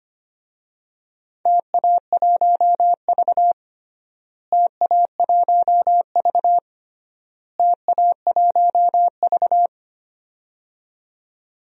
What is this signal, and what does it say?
Morse code, heard as a keyed tone, about 25 wpm, 700 Hz. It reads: TA1V TA1V TA1V
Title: DXCC-100 Call Signs - 4 Characters - Repeated 3x - 25wpm